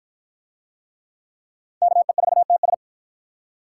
Morse code, heard as a keyed tone, about 52 wpm, 700 Hz. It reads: KE4TS